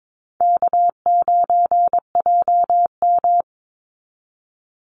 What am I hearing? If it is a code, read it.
Morse code, heard as a keyed tone, about 22 wpm, 700 Hz. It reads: K9JM
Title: Call Signs Repeated 2x - 22wpm